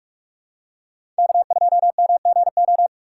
Morse code, heard as a keyed tone, about 46 wpm, 700 Hz. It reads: K2MGO
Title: USA Call Signs - 3x Character Spacing - 45wpm